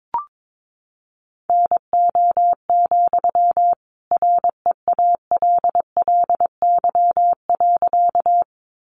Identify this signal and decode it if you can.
Morse code, heard as a keyed tone, about 22 wpm, 700 Hz. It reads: NO, REALLY.
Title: Sentences from Top 600 Words 22wpm